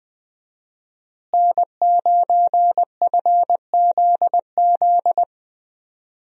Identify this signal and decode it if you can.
Morse code, heard as a keyed tone, about 20 wpm, 700 Hz. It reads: N9FZZ
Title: USA Call Signs - 5 Characters - Repeated 3x - 20wpm